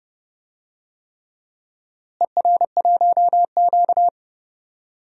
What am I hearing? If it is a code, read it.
Morse code, heard as a keyed tone, about 30 wpm, 700 Hz. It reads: ER1Q
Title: DXCC-100 Call Signs - 4 Characters - Repeated 3x - 30wpm